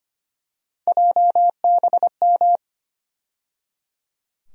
Morse code, heard as a keyed tone, about 25 wpm, 700 Hz. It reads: JBM
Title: USA Call Sign Postfixes - 3 Letters 25wpm